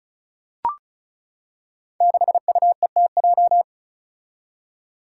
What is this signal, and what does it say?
Morse code, heard as a keyed tone, about 36 wpm, 700 Hz. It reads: 6UETJ
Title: Five Challenging Letter-Number Combinations 35wpm